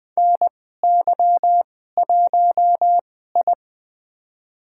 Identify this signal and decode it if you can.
Morse code, heard as a keyed tone, about 21 wpm, spaced as for 15 wpm, 700 Hz. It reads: NY1I
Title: USA Call Signs - 2x Character Spacing - 20wpm